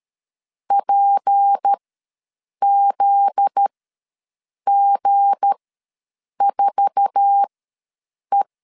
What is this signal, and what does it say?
Morse code, heard as a keyed tone, about 13 wpm, spaced as for 7 wpm, 800 Hz. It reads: PZG4E
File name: LEZIONI / 8 YJ45 / 8 YJ4 ws 12wpm 3249